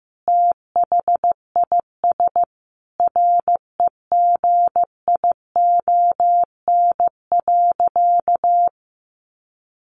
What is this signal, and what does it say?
Morse code, heard as a keyed tone, about 15 wpm, 700 Hz. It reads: THIS REGION.
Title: Sentences from Top 1500 Words 15wpm